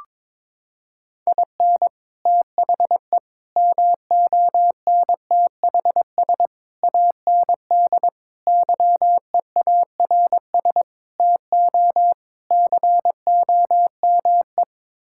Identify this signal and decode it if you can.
Morse code, heard as a keyed tone, about 22 wpm, 700 Hz. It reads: IN THE MONTHS AND YEARS TO COME
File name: practice / 22wpm / Sets of 7 Words - 22wpm